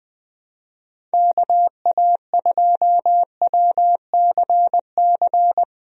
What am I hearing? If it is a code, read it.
Morse code, heard as a keyed tone, about 20 wpm, 700 Hz. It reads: KA2WCC
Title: USA Call Signs - 6 Characters - Repeated 3x - 20wpm